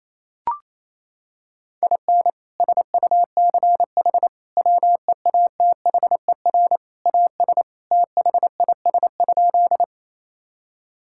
Morse code, heard as a keyed tone, about 28 wpm, 700 Hz. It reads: IN SUCH WEATHER AS THIS?
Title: Sentences from Top 1000 Words - Review Part 5 - 28wpm